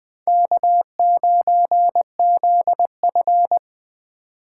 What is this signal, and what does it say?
Morse code, heard as a keyed tone, about 24 wpm, 700 Hz. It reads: K9ZF
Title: Speed Racing - USA Call Signs 20wpm